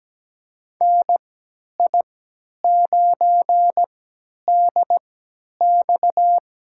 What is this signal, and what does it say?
Morse code, heard as a keyed tone, about 17 wpm, spaced as for 10 wpm, 700 Hz. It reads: NI9DX
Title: USA Call Signs - 3x Character Spacing - 17wpm